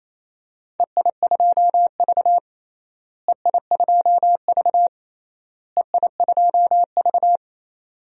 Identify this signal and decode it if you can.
Morse code, heard as a keyed tone, about 28 wpm, 700 Hz. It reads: EI2V EI2V EI2V
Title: DXCC-100 Call Signs - 4 Characters - Repeated 3x - 28wpm